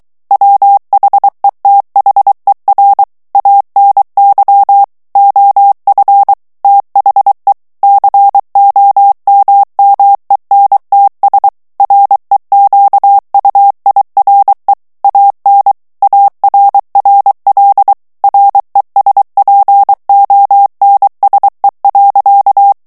Morse code, heard as a keyed tone, about 23 wpm, 800 Hz. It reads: WHETHER ANY OF THE COMMENTS REQUIRE AN ARRL RESPONSE.
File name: ANGLAIS texte clair 3 vitesse 22 mots minute